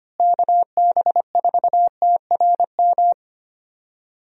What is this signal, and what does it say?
Morse code, heard as a keyed tone, about 25 wpm, 700 Hz. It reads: KB4TRM
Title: USA Call Signs - 6 Characters - Repeated 2x - 25wpm